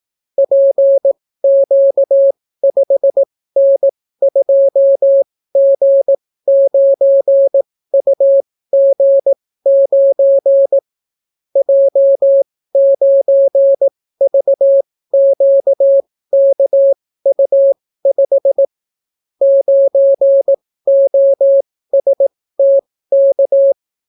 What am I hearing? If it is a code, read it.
Morse code, heard as a keyed tone, about 18 wpm, spaced as for 15 wpm, 550 Hz. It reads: PQ5N2G9UG9 J9VQKU5 9OSTK